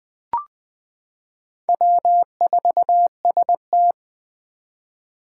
Morse code, heard as a keyed tone, about 21 wpm, 700 Hz. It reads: W4ST
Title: Four Challenging Letter-Number Combinations 20wpm